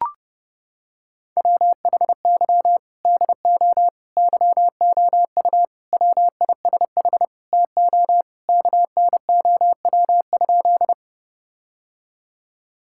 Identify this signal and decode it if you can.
Morse code, heard as a keyed tone, about 32 wpm, 700 Hz. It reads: WHY DO YOU WISH TO KNOW?
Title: Sentences from Top 1600 Words 30wpm